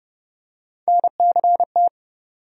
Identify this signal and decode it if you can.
Morse code, heard as a keyed tone, about 29 wpm, 700 Hz. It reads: NCT
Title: USA Call Sign Postfixes - 3 Letters 30wpm